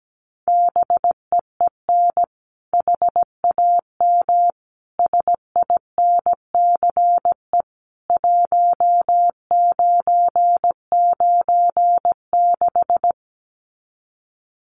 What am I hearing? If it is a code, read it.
Morse code, heard as a keyed tone, about 17 wpm, 700 Hz. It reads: BEEN HAM SINCE 1996
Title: Common QSO Phrases 17wpm